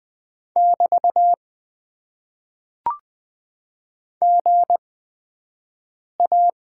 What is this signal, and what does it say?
Morse code, heard as a keyed tone, about 20 wpm, spaced as for 5 wpm, 700 Hz. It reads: <BT> GA
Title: Top 90 QSO Elements - 8x Character Spacing - 20wpm